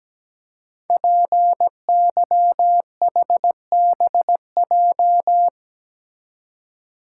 Morse code, heard as a keyed tone, about 17 wpm, 700 Hz. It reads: PYHBJ
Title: Code Groups - 5 Letters - 17wpm